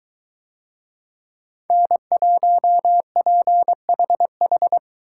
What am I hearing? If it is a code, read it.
Morse code, heard as a keyed tone, about 21 wpm, 700 Hz. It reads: N1PHH